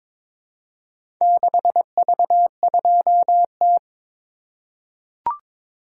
Morse code, heard as a keyed tone, about 22 wpm, 700 Hz. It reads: 6V2T
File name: practice / 22wpm / Four Letter-Number Combinations - 22wpm